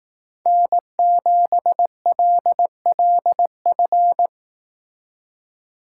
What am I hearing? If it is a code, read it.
Morse code, heard as a keyed tone, about 18 wpm, 700 Hz. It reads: N7LLF
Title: Speed Racing - USA 1x3 Call Signs 15wpm